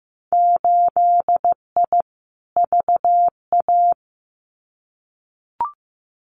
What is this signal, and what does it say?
Morse code, heard as a keyed tone, about 16 wpm, 700 Hz. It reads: 8I VA
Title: Winter Field Day Exchanges Encore 15wpm